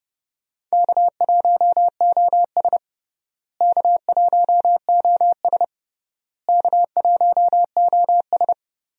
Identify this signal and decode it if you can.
Morse code, heard as a keyed tone, about 29 wpm, 700 Hz. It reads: K1OS K1OS K1OS
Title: USA Call Signs - 4 Characters - Repeated 3x - 30wpm